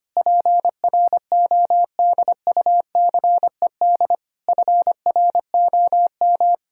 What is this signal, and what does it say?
Morse code, heard as a keyed tone, about 25 wpm, 700 Hz. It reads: PRODUCED FROM
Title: Sentences from Top 1000 Words - Review Part 6 - 25wpm